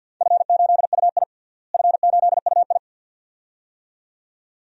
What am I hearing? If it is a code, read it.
Morse code, heard as a keyed tone, about 49 wpm, 700 Hz. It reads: F8UI F8UI
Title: DXCC-100 Call Signs - 4 Characters - Repeated 2x - 50wpm